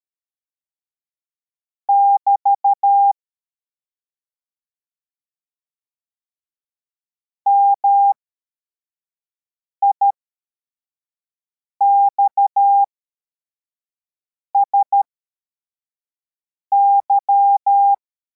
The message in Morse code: <BT> MIXSY